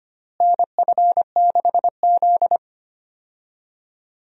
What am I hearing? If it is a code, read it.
Morse code, heard as a keyed tone, about 25 wpm, 700 Hz. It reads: NF6Z
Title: USA Call Signs - 4 Characters - Repeated 3x - 25wpm